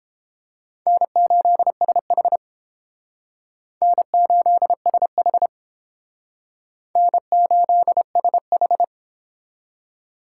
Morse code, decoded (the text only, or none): N8SH N8SH N8SH